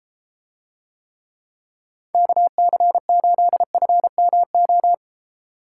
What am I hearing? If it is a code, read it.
Morse code, heard as a keyed tone, about 31 wpm, 700 Hz. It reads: KC8FMO